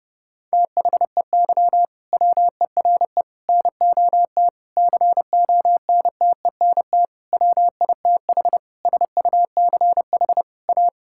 Morse code, heard as a keyed tone, about 30 wpm, 700 Hz. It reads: THEY WERE NOT CONTENT WITH SUCH A
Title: Sentences from Top 1000 Words - Review Part 6 - 30wpm